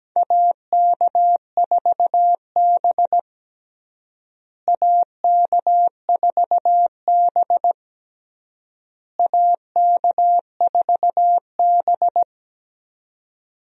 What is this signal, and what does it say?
Morse code, heard as a keyed tone, about 17 wpm, 700 Hz. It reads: AK4B AK4B AK4B
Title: USA Call Signs - 4 Characters - Repeated 3x - 17wpm